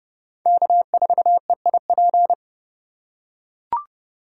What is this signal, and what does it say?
Morse code, heard as a keyed tone, about 30 wpm, 700 Hz. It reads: K4EIP
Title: USA Call Signs - 2x Character Spacing - 30wpm